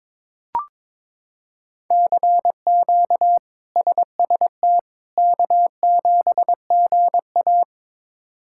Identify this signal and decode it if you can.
Morse code, heard as a keyed tone, about 22 wpm, 700 Hz. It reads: CQ SST K7GA
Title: K1USN SST QSO Practice - 22wpm